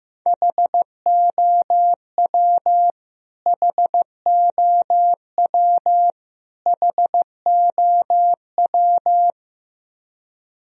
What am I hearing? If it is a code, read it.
Morse code, heard as a keyed tone, about 15 wpm, 700 Hz. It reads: HOW HOW HOW
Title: Top 100 Words Repeated 15wpm